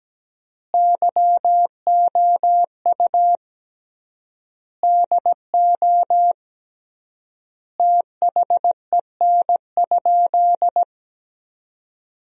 Morse code, heard as a keyed tone, about 17 wpm, 700 Hz. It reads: YOU DO THEN?
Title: 3x Word-Spacing - Sentences from Top 100 Words - 17wpm